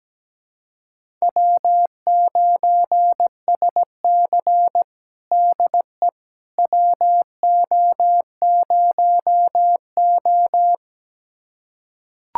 Morse code, heard as a keyed tone, about 17 wpm, 700 Hz. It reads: W9SC DE WO0O
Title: USA Callsign DE Callsign - 4 Characters - 17wpm